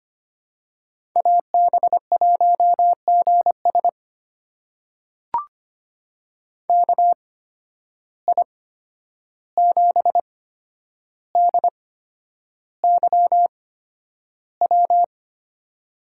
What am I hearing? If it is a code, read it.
Morse code, heard as a keyed tone, about 25 wpm, spaced as for 10 wpm, 700 Hz. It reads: AB1GS KI7DYW
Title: USA Call Signs - 8x Character Spacing - 25wpm